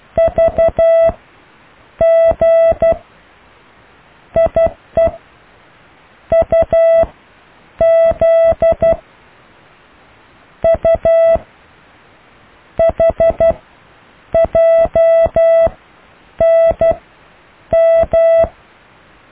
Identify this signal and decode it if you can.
Morse code, heard as a keyed tone, about 12 wpm, spaced as for 7 wpm, 660 Hz. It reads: VG IEUZ U HJNM
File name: cw / Alle letters